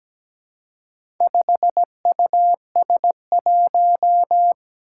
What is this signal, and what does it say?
Morse code, heard as a keyed tone, about 17 wpm, 700 Hz. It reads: 5US1
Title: Four Challenging Letter-Number Combinations 17wpm